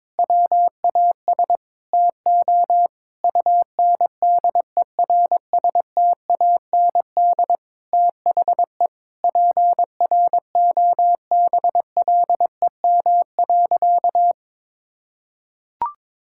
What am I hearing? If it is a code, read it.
Morse code, heard as a keyed tone, about 22 wpm, 700 Hz. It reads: WAS TO UNDERSTAND THE PROBLEM.
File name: Sentences with 7-Word Phrases - 22wpm